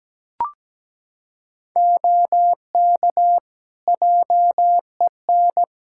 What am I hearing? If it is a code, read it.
Morse code, heard as a keyed tone, about 17 wpm, 700 Hz. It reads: OK JEN
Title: Practice QSO - Exchange 2 - Info - 17wpm